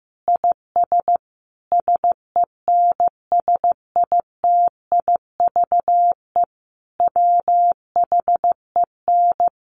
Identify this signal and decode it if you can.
Morse code, heard as a keyed tone, about 15 wpm, 700 Hz. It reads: IS SENSITIVE WHEN
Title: Sentences from Top 1400 Words 15wpm